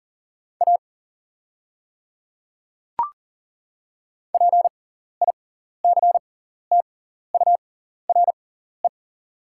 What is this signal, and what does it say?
Morse code, heard as a keyed tone, about 40 wpm, spaced as for 14 wpm, 690 Hz. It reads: A PICTURE
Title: 6x Character Spacing - Top 200 Words - 40wpm